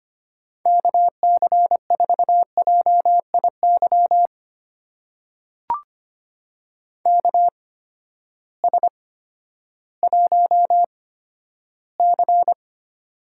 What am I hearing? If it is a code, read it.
Morse code, heard as a keyed tone, about 25 wpm, spaced as for 12 wpm, 700 Hz. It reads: KC4JIY KS1C
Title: USA Call Signs - 8x Character Spacing - 25wpm